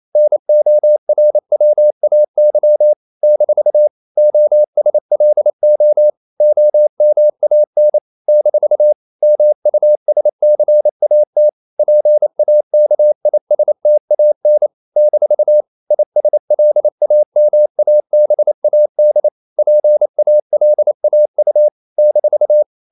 Morse code, heard as a keyed tone, about 28 wpm, 600 Hz. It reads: NORWAY - OSLO OMAN - MUSCAT PAKISTAN - ISLAMABAD PALAU -
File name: CW / Practice / NationalCapitals28